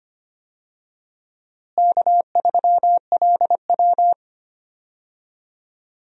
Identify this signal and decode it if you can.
Morse code, heard as a keyed tone, about 25 wpm, 700 Hz. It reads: K3LW
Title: USA Call Signs - 4 Characters - Repeated 3x - 25wpm